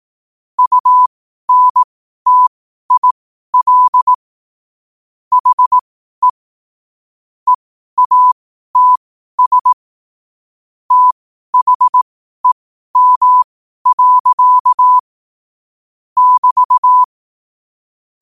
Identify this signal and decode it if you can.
Morse code, heard as a keyed tone, about 18 wpm, spaced as for 12 wpm, 1000 Hz. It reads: UNTIL HE EATS THEM. <BT>
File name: qotdfiles / 12wpm